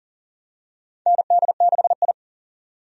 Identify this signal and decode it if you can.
Morse code, heard as a keyed tone, about 40 wpm, 700 Hz. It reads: ND6I
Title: USA Call Signs - 4 Characters - 40wpm